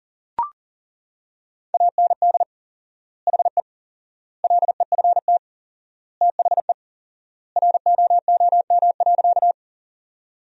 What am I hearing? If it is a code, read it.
Morse code, heard as a keyed tone, about 40 wpm, 700 Hz. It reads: AND HE LEFT THE ROOM.
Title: 4x Word-Spacing - Sentences from Top 300 Words - 40wpm